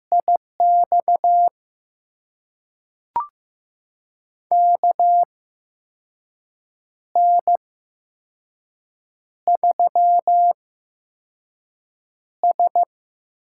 Morse code, heard as a keyed tone, about 15 wpm, spaced as for 5 wpm, 700 Hz. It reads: IX KN3S
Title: USA Call Signs - 4 Characters - 8x Character Spacing - 15wpm